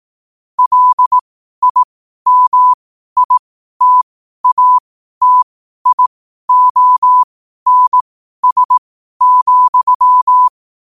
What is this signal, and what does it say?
Morse code, heard as a keyed tone, about 18 wpm, spaced as for 12 wpm, 1000 Hz. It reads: LIMITATIONS,
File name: qotdfiles / 12wpm